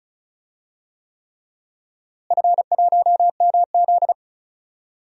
Morse code, heard as a keyed tone, about 36 wpm, 700 Hz. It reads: F1MZ
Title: DXCC-100 Call Signs - 4 Characters - Repeated 3x - 35wpm